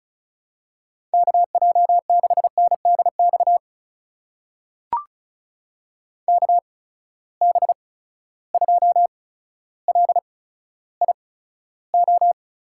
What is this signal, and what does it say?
Morse code, heard as a keyed tone, about 36 wpm, spaced as for 15 wpm, 700 Hz. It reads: KJ6NDX KB2LIO
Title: USA Call Signs - 6 Characters - 8x Character Spacing - 35wpm